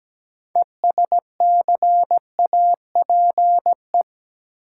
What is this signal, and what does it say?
Morse code, heard as a keyed tone, about 17 wpm, 700 Hz. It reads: ESCAPE
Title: Six Letter Words - 17wpm